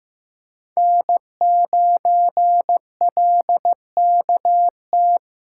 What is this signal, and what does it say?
Morse code, heard as a keyed tone, about 16 wpm, 700 Hz. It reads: N9LKT
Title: USA Callsign DE Callsign - 5 Characters - 15wpm